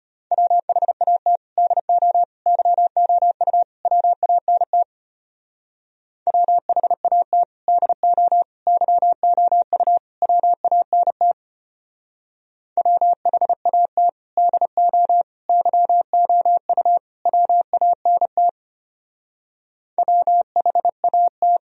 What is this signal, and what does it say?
Morse code, heard as a keyed tone, about 29 wpm, 700 Hz. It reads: WHAT DO YOU WANT WHAT DO YOU WANT WHAT DO YOU WANT WHAT
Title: Speed-Racing - Sets of 4 Words 25wpm